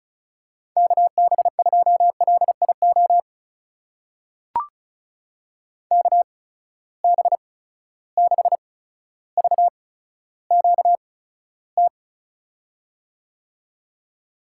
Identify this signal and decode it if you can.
Morse code, heard as a keyed tone, about 36 wpm, spaced as for 15 wpm, 700 Hz. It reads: KB2LIO KB6VQT
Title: USA Call Signs - 6 Characters - 8x Character Spacing - 35wpm